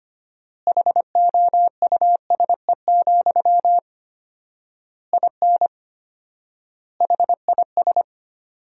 Morse code, heard as a keyed tone, about 25 wpm, 700 Hz. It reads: HOUSE, IN HIS
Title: 4x Word-Spacing - Sentences from Top 300 Words - 25wpm